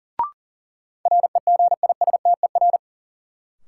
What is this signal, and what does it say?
Morse code, heard as a keyed tone, about 39 wpm, 700 Hz. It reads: REGISTER
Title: Sentences from Top 1200 Words - New Word Drill - 40wpm